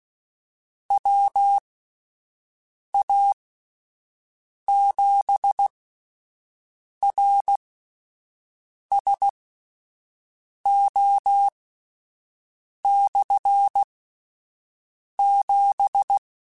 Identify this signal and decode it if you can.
Morse code, heard as a keyed tone, about 16 wpm, spaced as for 5 wpm, 800 Hz. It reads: WA7RSO/7
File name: Combinations / Combinations-mp3s / Combinations@5